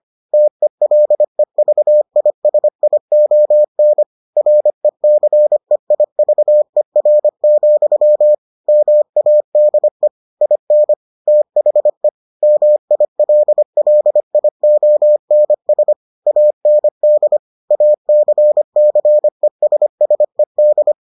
Morse code, heard as a keyed tone, about 25 wpm, 600 Hz. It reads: TELEVISION RECEIVER, MADE IN THE MILLIONS AND ACCESSED